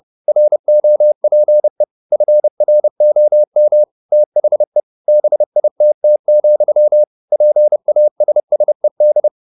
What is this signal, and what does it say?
Morse code, heard as a keyed tone, about 30 wpm, 600 Hz. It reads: ROPE FROM THE BITT, PASSED